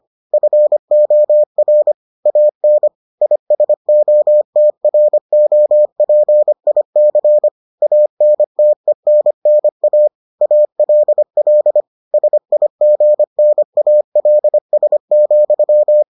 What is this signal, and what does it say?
Morse code, heard as a keyed tone, about 25 wpm, 600 Hz. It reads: FOR AN ISOTROPIC ANTENNA ALL SIGNALS,